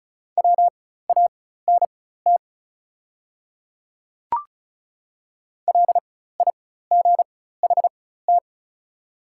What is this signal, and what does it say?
Morse code, heard as a keyed tone, about 36 wpm, spaced as for 16 wpm, 700 Hz. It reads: WANT LIGHT
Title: 4x Character Spacing - Top 200 Words - 35wpm